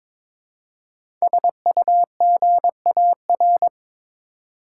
Morse code, heard as a keyed tone, about 22 wpm, 700 Hz. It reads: SUGAR
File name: 5 Letter Words - First 3 Letters Given - 22wpm